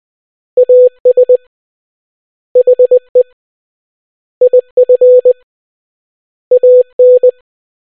Morse code, heard as a keyed tone, about 20 wpm, 500 Hz. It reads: AS HE IF AN